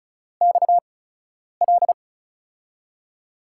Morse code, heard as a keyed tone, about 36 wpm, spaced as for 10 wpm, 700 Hz. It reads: XL
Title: USA Call Signs - 6 Characters - 8x Character Spacing - 35wpm